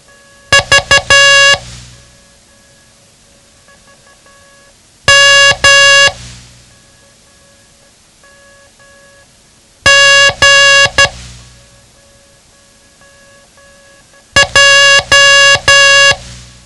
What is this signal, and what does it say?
Morse code, heard as a keyed tone, about 12 wpm, 1700 Hz. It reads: V M G J
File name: ham radio / CodeQuick / Code Quick 07